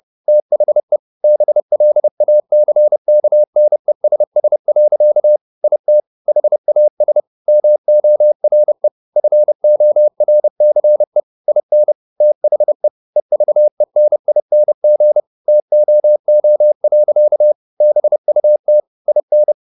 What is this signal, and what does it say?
Morse code, heard as a keyed tone, about 30 wpm, 600 Hz. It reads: THE BLACKNESS. IT HAS MORE FORCE IN THE EVENING TOO. BUT IN
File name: the old man and the sea / the old man and the sea 1 30wpm